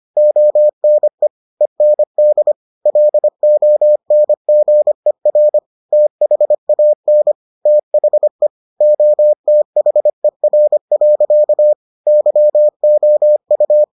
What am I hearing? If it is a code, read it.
Morse code, heard as a keyed tone, about 25 wpm, 600 Hz. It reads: ONE END LONGER THAN THE OTHER. YOU